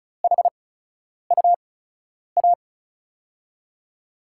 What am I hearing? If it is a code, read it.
Morse code, heard as a keyed tone, about 34 wpm, spaced as for 10 wpm, 700 Hz. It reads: HUA
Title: USA Call Signs - 5 Characters - 8x Character Spacing - 35wpm